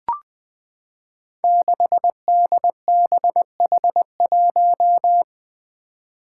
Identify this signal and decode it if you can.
Morse code, heard as a keyed tone, about 20 wpm, 700 Hz. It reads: 6DBH1
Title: Five Challenging Letter-Number Combinations 20wpm